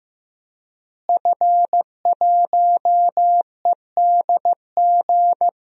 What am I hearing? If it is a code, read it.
Morse code, heard as a keyed tone, about 15 wpm, 700 Hz. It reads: F1EDG
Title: DXCC-100 Call Signs - 5 Characters - Repeated 2x - 15wpm